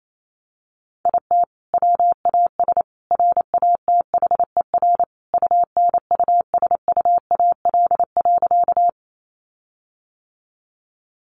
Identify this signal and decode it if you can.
Morse code, heard as a keyed tone, about 28 wpm, 700 Hz. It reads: IT WAS RATHER UNUSUAL.
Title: Sentences from Top 1400 Words 28wpm